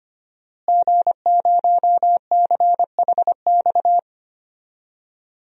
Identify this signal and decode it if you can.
Morse code, heard as a keyed tone, about 25 wpm, 700 Hz. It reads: G0CHX
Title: DXCC-100 Call Signs - 5 Characters - Repeated 2x - 25wpm